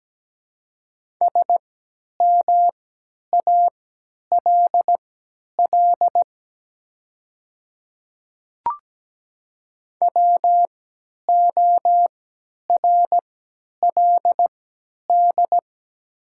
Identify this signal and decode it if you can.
Morse code, heard as a keyed tone, about 17 wpm, spaced as for 10 wpm, 700 Hz. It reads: SMALL WORLD
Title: 3x Character Spacing - Top 200 Words - 17wpm